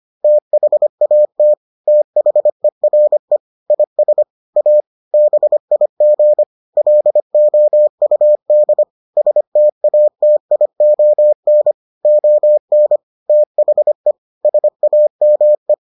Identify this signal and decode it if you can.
Morse code, heard as a keyed tone, about 25 wpm, 600 Hz. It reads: THAT THERE IS A BIG LOUD STATION ON THE SAME